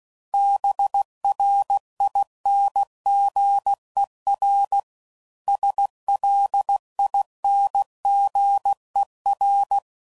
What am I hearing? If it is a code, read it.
Morse code, heard as a keyed tone, about 16 wpm, 800 Hz. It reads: BRINGER SLINGER
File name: Suffixes / Special INGs / Hidden Ings